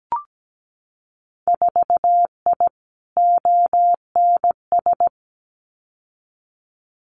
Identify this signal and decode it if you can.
Morse code, heard as a keyed tone, about 17 wpm, 700 Hz. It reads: 4I ONS
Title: Winter Field Day Exchanges Encore 17wpm